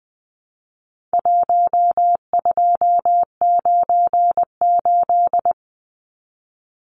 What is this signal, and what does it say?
Morse code, heard as a keyed tone, about 20 wpm, 700 Hz. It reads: 1298
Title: Numbers - 4-Digit 20wpm